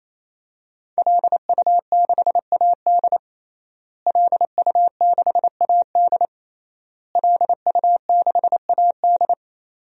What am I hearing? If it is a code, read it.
Morse code, heard as a keyed tone, about 28 wpm, 700 Hz. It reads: LU6AD LU6AD LU6AD
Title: DXCC-100 Call Signs - 5 Characters - Repeated 3x - 28wpm